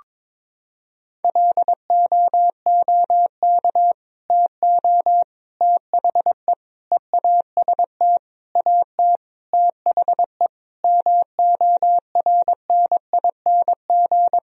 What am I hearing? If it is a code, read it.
Morse code, heard as a keyed tone, about 22 wpm, 700 Hz. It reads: LOOK TO THE EAST AT THE MORNING
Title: Sentences from Top 1000 Words - Review Part 3 - 22wpm